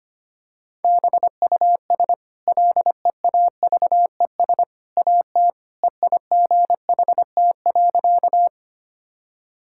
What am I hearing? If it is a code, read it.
Morse code, heard as a keyed tone, about 25 wpm, 690 Hz. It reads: BUS LEAVES AT EIGHT.
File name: Easy Sentences with Hints - 25wpm